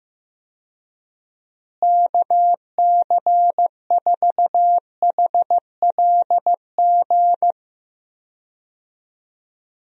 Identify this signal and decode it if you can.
Morse code, heard as a keyed tone, about 15 wpm, 700 Hz. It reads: KC4HLG
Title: USA Call Signs - 6 Characters - Repeated 3x - 15wpm